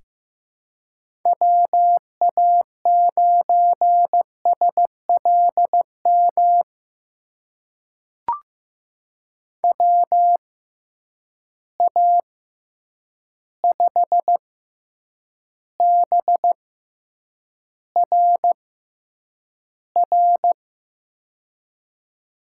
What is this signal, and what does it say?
Morse code, heard as a keyed tone, about 15 wpm, spaced as for 8 wpm, 700 Hz. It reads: WA9SLM WA5BRR
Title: USA Call Signs - 6 Characters - 6x Character Spacing - 15wpm